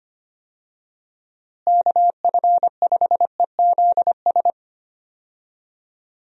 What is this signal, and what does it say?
Morse code, heard as a keyed tone, about 25 wpm, 700 Hz. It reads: KF5EZS
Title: USA Call Signs - 6 Characters - Repeated 3x - 25wpm